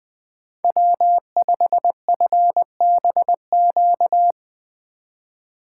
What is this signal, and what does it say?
Morse code, heard as a keyed tone, about 21 wpm, 700 Hz. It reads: W5FBQ